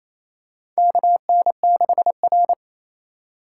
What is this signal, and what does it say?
Morse code, heard as a keyed tone, about 28 wpm, 700 Hz. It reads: KN6R
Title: USA Call Signs - 4 Characters - 4x Character Spacing - 28wpm